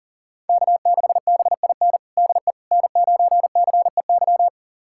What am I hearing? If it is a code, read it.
Morse code, heard as a keyed tone, about 40 wpm, 700 Hz. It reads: K6BIN DE N9CEY
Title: USA Callsign DE Callsign - 5 Characters - 40wpm